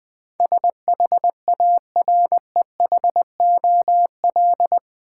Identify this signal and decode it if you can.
Morse code, heard as a keyed tone, about 20 wpm, 700 Hz. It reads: SHAREHOL
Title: Build A Word - 4 Letters At A Time - 11-Letter Words - 20wpm